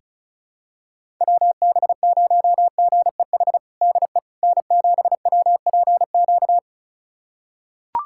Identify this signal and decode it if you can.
Morse code, heard as a keyed tone, about 36 wpm, 700 Hz. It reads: WB0GEH DE N7WPQ
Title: USA Callsign DE Callsign 35wpm